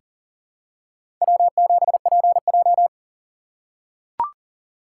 Morse code, heard as a keyed tone, about 40 wpm, 690 Hz. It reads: W7PJ
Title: USA Call Signs - 4 Characters - 2x Character Spacing - 40wpm